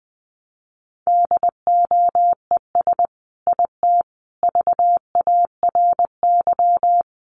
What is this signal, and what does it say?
Morse code, heard as a keyed tone, about 20 wpm, 700 Hz. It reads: DOES IT VARY
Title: Sentences from Top 1200 Words 20wpm